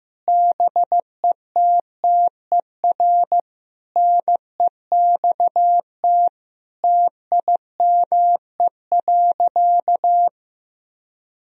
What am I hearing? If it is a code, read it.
Morse code, heard as a keyed tone, about 15 wpm, 700 Hz. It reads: BETTER NEXT TIME.
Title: Sentences from Top 1000 Words - Review Part 1 - 15wpm